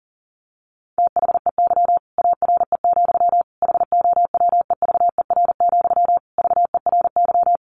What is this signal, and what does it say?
Morse code, heard as a keyed tone, about 40 wpm, 700 Hz. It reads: THEY ARE, HOWEVER, VERY
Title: Sentences from Top 1200 Words 40wpm